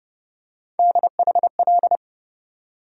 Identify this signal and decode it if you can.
Morse code, heard as a keyed tone, about 30 wpm, 700 Hz. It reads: DHL
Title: USA Call Sign Postfixes - 3 Letters 30wpm